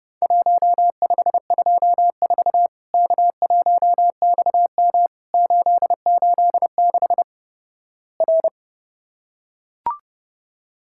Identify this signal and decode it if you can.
Morse code, heard as a keyed tone, about 30 wpm, 700 Hz. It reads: 1524 K1XM 886 R
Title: Worked All Europe DX Contest - QTC Exchange - 30wpm